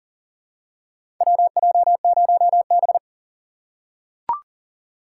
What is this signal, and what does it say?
Morse code, heard as a keyed tone, about 40 wpm, 700 Hz. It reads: WJ0B
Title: USA Call Signs - 6x Character Spacing - 40wpm